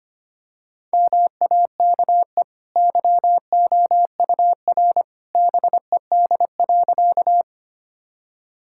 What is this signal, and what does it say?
Morse code, heard as a keyed tone, about 25 wpm, 700 Hz. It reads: MAKE YOUR BED.
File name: Easy Sentences with Hints - 25wpm